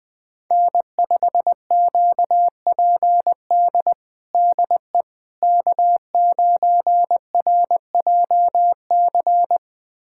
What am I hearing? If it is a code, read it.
Morse code, heard as a keyed tone, about 21 wpm, 700 Hz. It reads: N5QPD DE K9RJC
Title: USA Callsign DE Callsign - 5 Characters - 20wpm